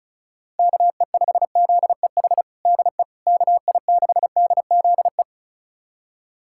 Morse code, heard as a keyed tone, about 36 wpm, 690 Hz. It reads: KE5ZEH DE KI6DZE
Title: USA Callsign DE Callsign 35wpm